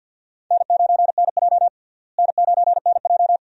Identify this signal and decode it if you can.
Morse code, heard as a keyed tone, about 54 wpm, 700 Hz. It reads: N9NJ N9NJ
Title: USA Call Signs - 4 Characters - Repeated 2x - 50wpm